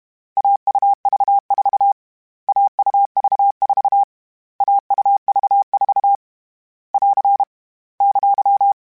Morse code, heard as a keyed tone, about 32 wpm, 800 Hz. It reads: AUV4 AUV4 AUV4 <AR> !